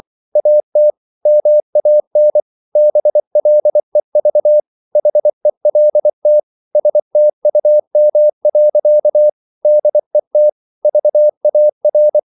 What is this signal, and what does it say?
Morse code, heard as a keyed tone, about 24 wpm, 600 Hz. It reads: AT MAN BLEV HELT STUM. DET VAR
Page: Ferie 24 wpm